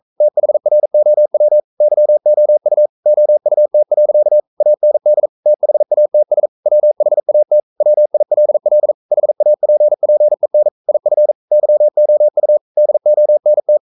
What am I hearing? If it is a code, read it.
Morse code, heard as a keyed tone, about 42 wpm, 600 Hz. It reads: THROW YOU OUT. AND THATS WHAT WILL HAPPEN IF YOU DONT
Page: Romanen Never af Ken Follet på engelsk.